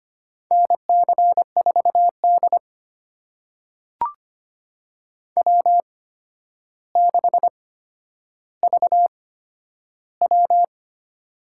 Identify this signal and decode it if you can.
Morse code, heard as a keyed tone, about 25 wpm, spaced as for 11 wpm, 700 Hz. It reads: NC4D W6VW
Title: USA Call Signs - 4 Characters - 8x Character Spacing - 25wpm